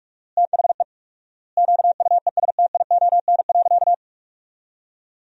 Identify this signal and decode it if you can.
Morse code, heard as a keyed tone, about 41 wpm, 700 Hz. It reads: THE QUESTION.
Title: 4x Word-Spacing - Sentences from Top 400 Words - 45wpm